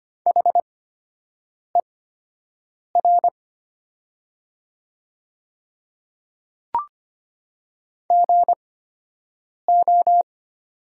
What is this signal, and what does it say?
Morse code, heard as a keyed tone, about 25 wpm, spaced as for 7 wpm, 700 Hz. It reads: HER GO
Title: 8x Character Spacing - Top 100 Words - 25wpm